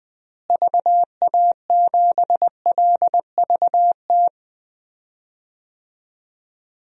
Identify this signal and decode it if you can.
Morse code, heard as a keyed tone, about 21 wpm, 700 Hz. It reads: VA7LVT
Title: DXCC-100 Call Signs - 6 Characters - Repeated 3x - 20wpm